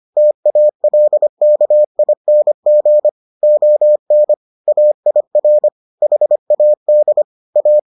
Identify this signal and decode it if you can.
Morse code, heard as a keyed tone, about 25 wpm, 600 Hz. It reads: TALKING ON AIR HAD A